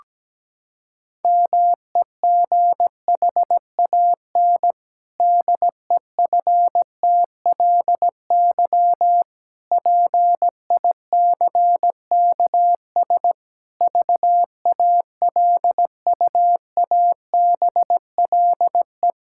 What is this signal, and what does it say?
Morse code, heard as a keyed tone, about 17 wpm, 700 Hz. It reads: MEGHAN DEFTLY PICKS VALUABLE
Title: Pangram Challenge v2 - 17wpm